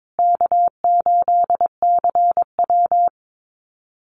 Morse code, heard as a keyed tone, about 22 wpm, 700 Hz. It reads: K8CW
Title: USA Call Signs Repeated 3x 22wpm